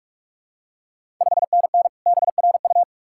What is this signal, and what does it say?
Morse code, heard as a keyed tone, about 46 wpm, 700 Hz. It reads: 5NN BRU
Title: Ontario QSO Party Exchange 45wpm